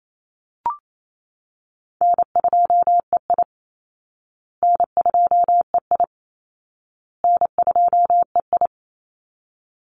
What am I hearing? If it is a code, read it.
Morse code, heard as a keyed tone, about 28 wpm, 700 Hz. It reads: N2EI N2EI N2EI
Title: Call Signs Repeated 2x - 28wpm